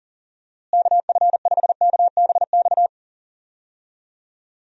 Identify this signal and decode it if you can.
Morse code, heard as a keyed tone, about 40 wpm, 700 Hz. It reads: KF5KBX
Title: USA Call Signs - 6 Characters - Repeated 2x - 40wpm